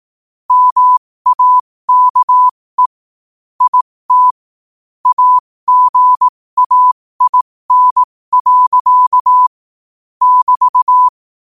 Morse code, heard as a keyed tone, about 18 wpm, spaced as for 15 wpm, 1000 Hz. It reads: MAKE IT AGAIN. <BT>
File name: 2024-11-18-15wpm